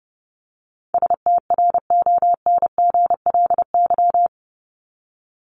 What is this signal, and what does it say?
Morse code, heard as a keyed tone, about 30 wpm, 700 Hz. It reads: STRONGLY
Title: LY Suffix - Top 100 Words with Base Words 30wpm